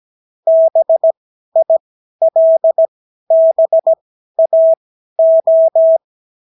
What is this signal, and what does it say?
Morse code, heard as a keyed tone, about 17 wpm, spaced as for 12 wpm, 650 Hz. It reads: BILBAO
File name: CW QSO libroCW4 18-8ppm